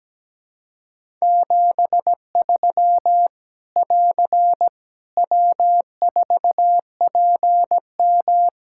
Morse code, heard as a keyed tone, about 17 wpm, 700 Hz. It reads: 73 <AR> W4PM
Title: Practice QSO - Exchange 4 - Ending - 17wpm